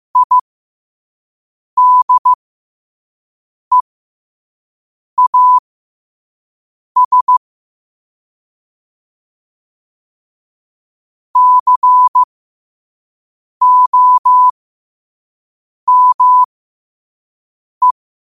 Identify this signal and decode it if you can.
Morse code, heard as a keyed tone, about 15 wpm, spaced as for 5 wpm, 1000 Hz. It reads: IDEAS COME
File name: qotdfiles / 5wpm